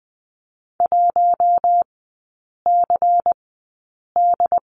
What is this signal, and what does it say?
Morse code, heard as a keyed tone, about 20 wpm, spaced as for 8 wpm, 700 Hz. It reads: 1CD
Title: 191 Lesson 31e - Letter v - Three Characters 20wpm